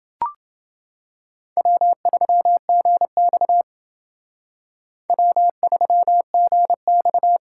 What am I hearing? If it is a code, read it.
Morse code, heard as a keyed tone, about 28 wpm, 690 Hz. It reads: W3GX W3GX